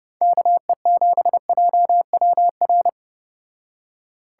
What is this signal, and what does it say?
Morse code, heard as a keyed tone, about 29 wpm, 700 Hz. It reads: KE7JWR